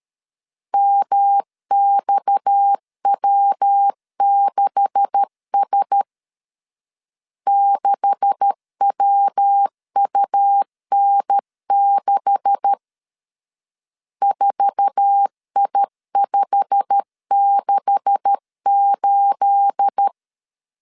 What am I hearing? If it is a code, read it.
Morse code, heard as a keyed tone, about 13 wpm, 800 Hz. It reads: MXW6S 6WUN6 4I568
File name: LEZIONI / 9 286 12wpm 2305